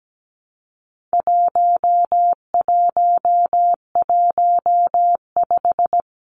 Morse code, heard as a keyed tone, about 17 wpm, 700 Hz. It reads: 1115
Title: Numbers - 4-Digit 17wpm